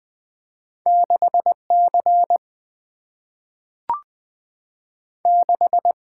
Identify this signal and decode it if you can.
Morse code, heard as a keyed tone, about 20 wpm, 700 Hz. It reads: 6 C 6
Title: Two Letter-Number Combinations - 20wpm